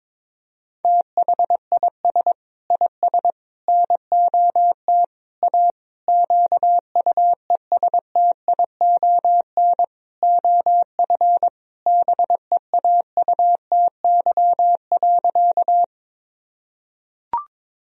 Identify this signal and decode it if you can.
Morse code, heard as a keyed tone, about 22 wpm, 700 Hz. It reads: THIS IS NOT A QUESTION OF BEAUTY.
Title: Sentences from Top 1000 Words - Review Part 2 - 22wpm